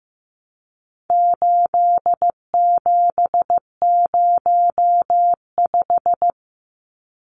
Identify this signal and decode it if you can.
Morse code, heard as a keyed tone, about 15 wpm, 700 Hz. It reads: 8705
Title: Numbers - 4-Digit 15wpm